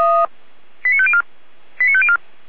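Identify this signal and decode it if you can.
Five-tone selective call (ZVEI), heard as quick runs of five tones, tones of about 70 ms. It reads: 89504 89504
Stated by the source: ZVEI five-tone call received on SDR, from the Feuerwehr Orlamünde fire brigade audio server.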